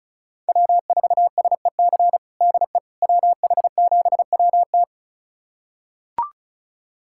Morse code, heard as a keyed tone, about 34 wpm, 700 Hz. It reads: W4SEC DE WH7WT
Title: USA Callsign DE Callsign - 5 Characters - 35wpm